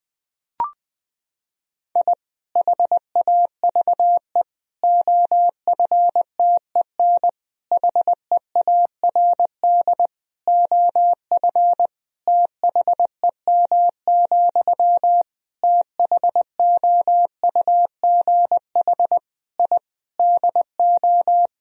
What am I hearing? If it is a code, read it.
Morse code, heard as a keyed tone, about 20 wpm, 700 Hz. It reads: I HAVE OFTEN HEARD OF THEM, THOUGH I DO
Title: Sentences from Top 1000 Words - Review Part 2 - 20wpm